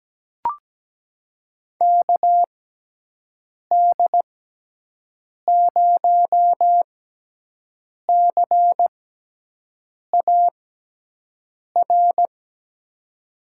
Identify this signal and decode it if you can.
Morse code, heard as a keyed tone, about 17 wpm, spaced as for 6 wpm, 700 Hz. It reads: KD0CAR
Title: USA Call Signs - 6 Characters - 6x Character Spacing - 17wpm